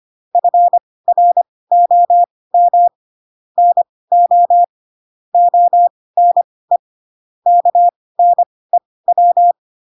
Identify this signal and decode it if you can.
Morse code, heard as a keyed tone, about 25 wpm, spaced as for 18 wpm, 700 Hz. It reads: FROM NO ONE KNEW
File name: Pinocchio / Pinocchio ch2 12 25 ik1hge